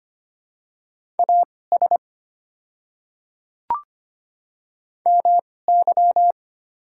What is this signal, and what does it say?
Morse code, heard as a keyed tone, about 25 wpm, spaced as for 18 wpm, 700 Hz. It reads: AS MY
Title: 2x Character Spacing - Top 100 Words - 25wpm